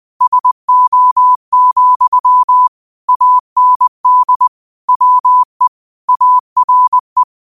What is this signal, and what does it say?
Morse code, heard as a keyed tone, about 20 wpm, 1000 Hz. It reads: SO, AND WE ARE